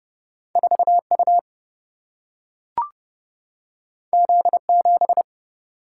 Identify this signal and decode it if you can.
Morse code, heard as a keyed tone, about 30 wpm, 700 Hz. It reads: 4U Z7
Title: Single-Letter Single-Number 30wpm